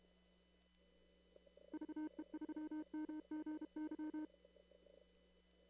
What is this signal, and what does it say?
Morse code, heard as a keyed tone, about 32 wpm, 310 Hz. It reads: VE3MGY
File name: CW / kt5j wpx cw-op-k5pi